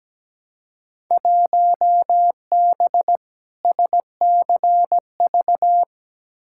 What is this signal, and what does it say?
Morse code, heard as a keyed tone, about 17 wpm, 700 Hz. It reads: 1B SCV
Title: Field Day Exchanges - 17wpm